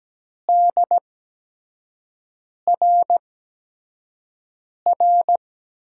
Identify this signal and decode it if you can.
Morse code, heard as a keyed tone, about 17 wpm, spaced as for 5 wpm, 700 Hz. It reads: DRR
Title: USA Call Signs - 5 Characters - 8x Character Spacing - 17wpm